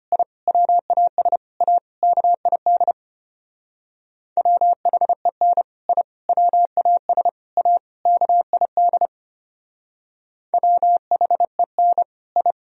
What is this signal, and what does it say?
Morse code, heard as a keyed tone, about 31 wpm, 700 Hz. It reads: I WAS A KID WHEN I WAS A KID WHEN I
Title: Speed-Racing - Sets of 5 Words 25wpm